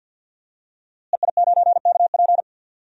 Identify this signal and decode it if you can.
Morse code, heard as a keyed tone, about 52 wpm, 700 Hz. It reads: EI9KP
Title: DXCC-100 Call Signs - 5 Characters - Repeated 2x - 50wpm